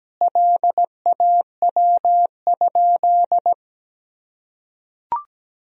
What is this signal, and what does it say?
Morse code, heard as a keyed tone, about 17 wpm, 700 Hz. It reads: LAW?
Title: Sentences from Top 1000 Words - Review Part 4 - 17wpm